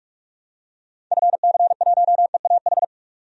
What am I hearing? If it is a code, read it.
Morse code, heard as a keyed tone, about 43 wpm, 700 Hz. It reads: FC1EAH
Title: DXCC-100 Call Signs - 6 Characters - 45wpm